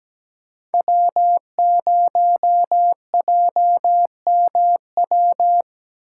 Morse code, heard as a keyed tone, about 17 wpm, 700 Hz. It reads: W0JMW
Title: USA Call Signs - 8x Character Spacing - 17wpm